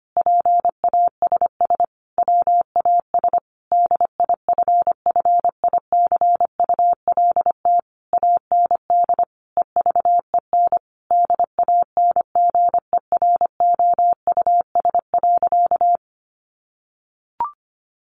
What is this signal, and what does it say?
Morse code, heard as a keyed tone, about 25 wpm, 700 Hz. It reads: PASS WAS DIFFICULT AND EVEN DANGEROUS.
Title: Sentences from Top 1200 Words 25wpm